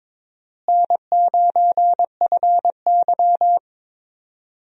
Morse code, heard as a keyed tone, about 22 wpm, 700 Hz. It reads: N9FY
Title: USA Call Signs - 4 Characters - Repeated 2x - 22wpm